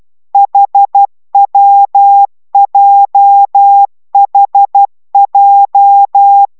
Morse code, heard as a keyed tone, about 12 wpm, 800 Hz. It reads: HWJHJ